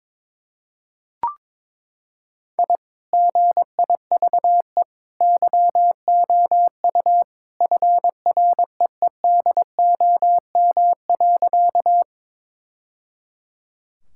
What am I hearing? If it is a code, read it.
Morse code, heard as a keyed tone, about 22 wpm, 700 Hz. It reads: I GIVE YOU FREEDOM.
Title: Sentences from Top 1000 Words - Review Part 5 - 22wpm